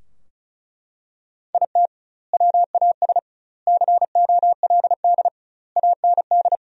Morse code, heard as a keyed tone, about 34 wpm, 700 Hz. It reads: IT WAS COLD AND
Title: 2x Word-Spacing - Sentences from Top 500 Words - 35wpm